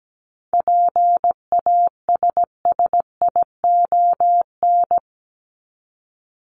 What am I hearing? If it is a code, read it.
Morse code, heard as a keyed tone, about 17 wpm, 700 Hz. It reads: PASSION
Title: Top 1001-1250 Words 17wpm